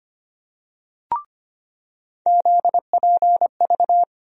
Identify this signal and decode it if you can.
Morse code, heard as a keyed tone, about 25 wpm, 700 Hz. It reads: ZPV
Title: USA Call Sign Postfixes - 3 Letters 25wpm